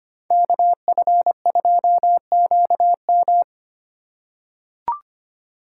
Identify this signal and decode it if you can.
Morse code, heard as a keyed tone, about 25 wpm, 700 Hz. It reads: KF2QM
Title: USA Call Signs - 5 Characters - 3x Character Spacing - 25wpm